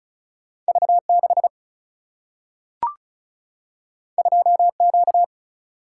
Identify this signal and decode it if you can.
Morse code, heard as a keyed tone, about 36 wpm, 700 Hz. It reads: V 6 2 Q
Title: Single-Letter Single-Number 35wpm